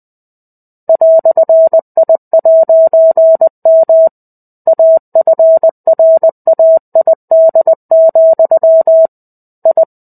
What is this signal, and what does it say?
Morse code, heard as a keyed tone, about 20 wpm, 650 Hz. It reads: "I'M AFRAID, I
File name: 05-Chapter-05 20 WPM